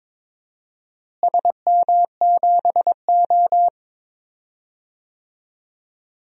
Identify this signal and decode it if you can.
Morse code, heard as a keyed tone, about 21 wpm, 700 Hz. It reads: SM7O
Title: DXCC-100 Call Signs - 4 Characters - Repeated 2x - 22wpm